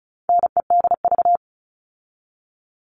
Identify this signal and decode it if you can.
Morse code, heard as a keyed tone, about 34 wpm, 700 Hz. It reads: NEDV
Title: All Cut Numbers - 4-Digit 35wpm